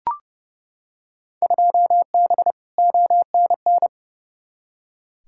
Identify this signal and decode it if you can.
Morse code, heard as a keyed tone, about 30 wpm, 700 Hz. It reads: 2B ONN
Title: Field Day Exchanges Encore - 30wpm